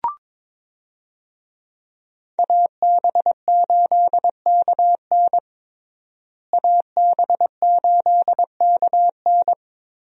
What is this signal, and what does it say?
Morse code, heard as a keyed tone, about 22 wpm, 700 Hz. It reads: AB8KN AB8KN
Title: USA Call Signs - 5 Characters - Repeated 2x - 22wpm